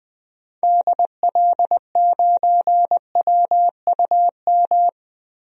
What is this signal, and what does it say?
Morse code, heard as a keyed tone, about 20 wpm, 700 Hz. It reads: DL9WUM
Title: DXCC-100 Call Signs - 6 Characters - Repeated 2x - 20wpm